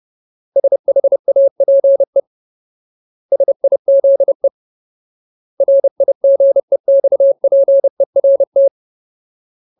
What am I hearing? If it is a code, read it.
Morse code, heard as a keyed tone, about 30 wpm, 550 Hz. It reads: SHAPE SIZE RIGEXPERT
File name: Ultimate Word List / 30WPM Ultimate